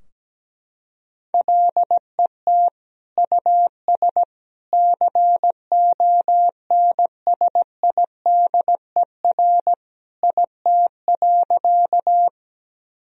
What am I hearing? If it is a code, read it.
Morse code, heard as a keyed tone, about 17 wpm, 700 Hz. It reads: LET US CONSIDER IT.
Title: Sentences from Top 1000 Words - Review Part 4 - 17wpm